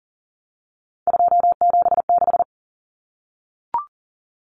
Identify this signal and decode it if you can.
Morse code, heard as a keyed tone, about 40 wpm, 700 Hz. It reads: 276
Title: Numbers - 3-Digit 40wpm